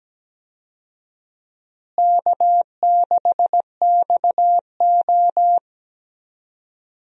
Morse code, heard as a keyed tone, about 17 wpm, 700 Hz. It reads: K6XO
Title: USA Call Signs - 4 Characters - Repeated 3x - 17wpm